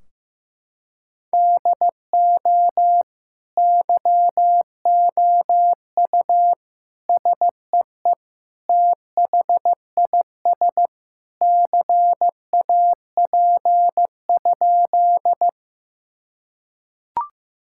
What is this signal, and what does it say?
Morse code, heard as a keyed tone, about 15 wpm, 700 Hz. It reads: DO YOU SEE THIS CAP?
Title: Sentences from Top 1600 Words 15wpm